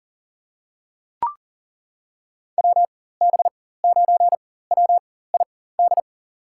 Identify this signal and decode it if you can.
Morse code, heard as a keyed tone, about 37 wpm, spaced as for 18 wpm, 700 Hz. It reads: WB9WID
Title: USA Call Signs - 6 Characters - 4x Character Spacing - 40wpm